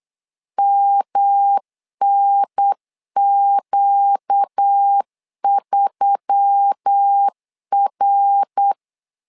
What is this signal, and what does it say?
Morse code, heard as a keyed tone, about 8 wpm, 800 Hz. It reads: MNQ3R
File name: LEZIONI / 4 Q1S3 8wpm 3037